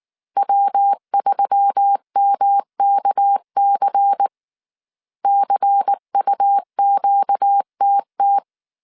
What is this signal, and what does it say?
Morse code, heard as a keyed tone, about 19 wpm, 800 Hz. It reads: W3MKC CUQTT
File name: CW / DET / 1 6 1500C S C 18wpm 2356